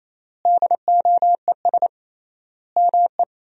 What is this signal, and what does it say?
Morse code, heard as a keyed tone, about 28 wpm, 700 Hz. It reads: DOES ME
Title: 3x Word-Spacing - Sentences from Top 200 Words - 28wpm